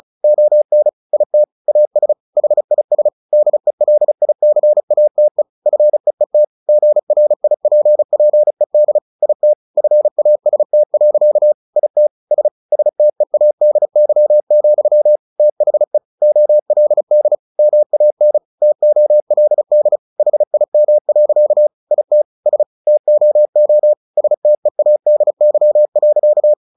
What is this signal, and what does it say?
Morse code, heard as a keyed tone, about 34 wpm, 600 Hz. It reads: ON IT AS HIS DELICATE FEET GRIPPED IT FAST. IT S STEADY, THE OLD MAN TOLD HIM. IT S TOO STEADY.
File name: the old man and the sea / the old man and the sea 1 35wpm